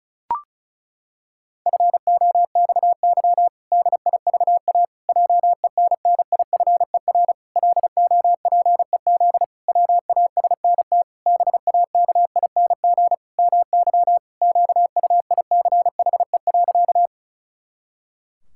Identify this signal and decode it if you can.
Morse code, heard as a keyed tone, about 34 wpm, 700 Hz. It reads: FOXY DIVA JENNIFER LOPEZ WASNT BAKING MY QUICHE.
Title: Pangram Challenge v2 - 35wpm